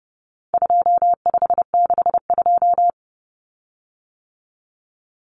Morse code, heard as a keyed tone, about 30 wpm, 700 Hz. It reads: 2562
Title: Common Cut Numbers - 4-Digit 30wpm